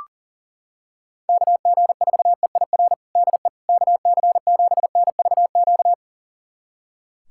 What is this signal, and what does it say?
Morse code, heard as a keyed tone, about 37 wpm, 700 Hz. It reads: KG4EIR DE KC7NVQ
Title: USA Callsign DE Callsign 40wpm